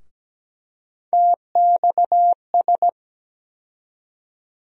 Morse code, heard as a keyed tone, about 17 wpm, 700 Hz. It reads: TXS
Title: USA Call Sign Postfixes - 3 Letters 17wpm